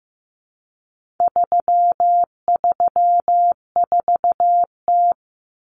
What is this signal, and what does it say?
Morse code, heard as a keyed tone, about 15 wpm, 700 Hz. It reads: 334T
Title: Common Cut Numbers - 4-Digit 15wpm